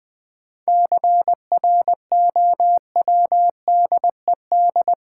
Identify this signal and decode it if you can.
Morse code, heard as a keyed tone, about 20 wpm, 700 Hz. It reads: CROWDED
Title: 7 Letter Words - First 3 Letters Given - 20wpm